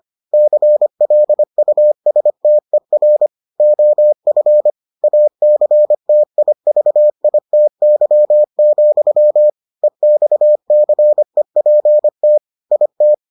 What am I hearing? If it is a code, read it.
Morse code, heard as a keyed tone, about 25 wpm, 600 Hz. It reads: CLUSTER OF ACTIVITY, EXCEPT IT